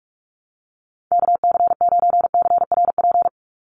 Morse code, heard as a keyed tone, about 44 wpm, 700 Hz. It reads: KC9CRP
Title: USA Call Signs 45wpm